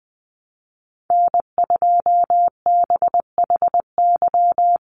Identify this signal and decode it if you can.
Morse code, heard as a keyed tone, about 20 wpm, 700 Hz. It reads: N2BHY
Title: USA Call Signs 20wpm